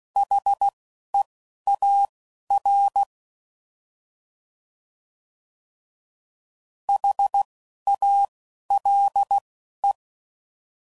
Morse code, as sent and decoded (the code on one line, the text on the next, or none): .... .- .-. / .... .- .-.. .
HEAR HALE